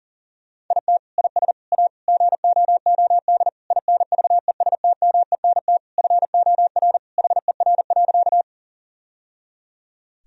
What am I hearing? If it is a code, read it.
Morse code, heard as a keyed tone, about 40 wpm, 700 Hz. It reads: IT IS A GOOD INVESTMENT FOR HER.
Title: Sentences from Top 1000 Words - Review Part 4 - 40wpm